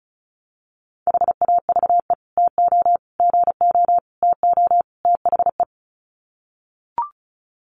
Morse code, heard as a keyed tone, about 34 wpm, 700 Hz. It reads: HAVE TO GO TO THE
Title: Sets of 5 Words Encore 35wpm